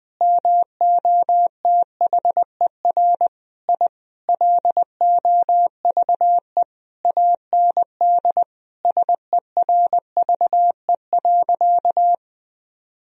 Morse code, heard as a keyed tone, about 20 wpm, 700 Hz. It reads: MOTHER I LOVE AND SERVE.
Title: Sentences from Top 1000 Words - Review Part 2 - 20wpm